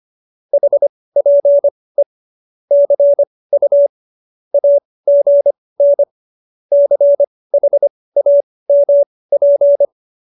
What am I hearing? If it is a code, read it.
Morse code, heard as a keyed tone, about 25 wpm, spaced as for 18 wpm, 570 Hz. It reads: HPE CU AGN CHAMP